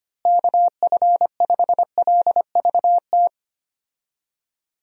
Morse code, heard as a keyed tone, about 25 wpm, 700 Hz. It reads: KF5LVT
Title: USA Call Signs - 6 Characters - Repeated 3x - 25wpm